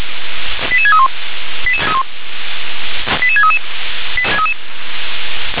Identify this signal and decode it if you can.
Five-tone selective call (ZVEI), heard as quick runs of five tones, tones of about 70 ms. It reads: 88631 88633